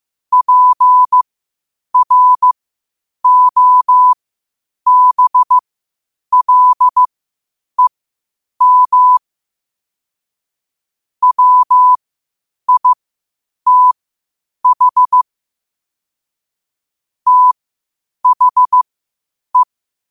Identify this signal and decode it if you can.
Morse code, heard as a keyed tone, about 15 wpm, spaced as for 8 wpm, 1000 Hz. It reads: PROBLEM WITH THE
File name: qotdfiles / 8wpm